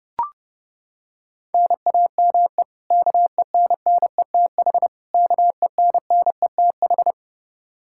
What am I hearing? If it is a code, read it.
Morse code, heard as a keyed tone, about 32 wpm, 700 Hz. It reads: NAME KENNETH KENNETH
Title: Practice QSO - Exchange 1 - Beginning - 30wpm